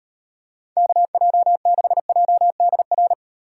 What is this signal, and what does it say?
Morse code, heard as a keyed tone, about 35 wpm, 700 Hz. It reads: KJ6JDR